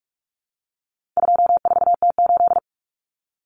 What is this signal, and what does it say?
Morse code, heard as a keyed tone, about 44 wpm, 700 Hz. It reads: 24T8
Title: Common Cut Numbers - 4-Digit 45wpm